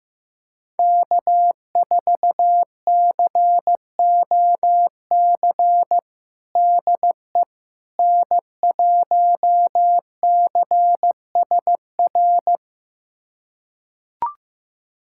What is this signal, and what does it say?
Morse code, heard as a keyed tone, about 15 wpm, 700 Hz. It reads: K4COC DE N1CSR
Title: USA Callsign DE Callsign 15wpm